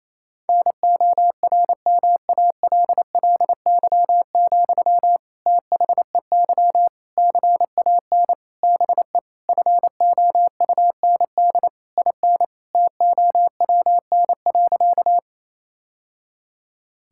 Morse code, heard as a keyed tone, about 27 wpm, 700 Hz. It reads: NORMALLY, THEY CAN BE FOUND IN TOWN.
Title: Sentences from Top 1000 Words - Review Part 6 - 28wpm